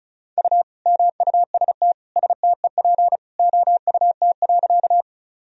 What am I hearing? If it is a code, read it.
Morse code, heard as a keyed tone, about 36 wpm, 700 Hz. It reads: U MUST STEP OUT.
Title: Sentences from Top 400 Words and Abbreviations 35wpm